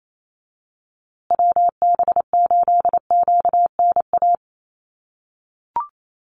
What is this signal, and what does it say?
Morse code, heard as a keyed tone, about 28 wpm, 700 Hz. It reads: WB8QNA
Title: USA Call Signs Repeated 3x 28wpm